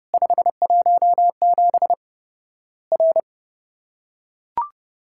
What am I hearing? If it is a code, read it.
Morse code, heard as a keyed tone, about 30 wpm, 700 Hz. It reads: 517 R
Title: Worked All Europe DX Contest - QTC Exchange - 30wpm